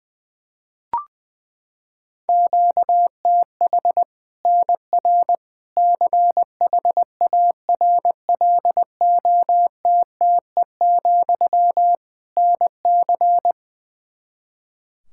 Morse code, heard as a keyed tone, about 20 wpm, 700 Hz. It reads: QTH NR CHARLOTTE, NC
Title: Practice QSO - Exchange 1 - Beginning - 20wpm